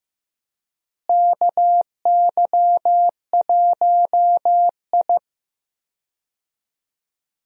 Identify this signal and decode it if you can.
Morse code, heard as a keyed tone, about 16 wpm, 700 Hz. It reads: KY1I
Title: USA Call Signs - 4 Characters - Repeated 2x - 15wpm